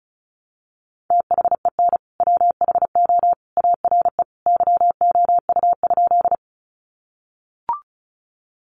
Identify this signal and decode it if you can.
Morse code, heard as a keyed tone, about 34 wpm, 700 Hz. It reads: THEN WHO ARE YOU?
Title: Sentences from Top 100 Words 35wpm